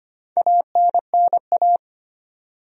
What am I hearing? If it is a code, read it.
Morse code, heard as a keyed tone, about 25 wpm, 700 Hz. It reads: ANNA
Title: USA Common Names - Max 4 Letters - 25wpm